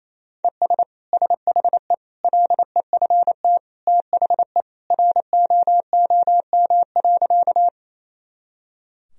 Morse code, heard as a keyed tone, about 28 wpm, 700 Hz. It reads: ES SHE LEFT THE ROOM.
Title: Sentences from Top 300 Words and Abbreviations 28wpm